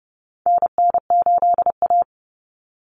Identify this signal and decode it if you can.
Morse code, heard as a keyed tone, about 30 wpm, 700 Hz. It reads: NN8A
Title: Common Cut Numbers - 4-Digit 30wpm